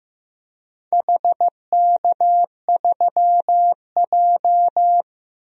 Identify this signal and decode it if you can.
Morse code, heard as a keyed tone, about 15 wpm, 700 Hz. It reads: HK3J
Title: DXCC-100 Call Signs - 4 Characters - Repeated 3x - 15wpm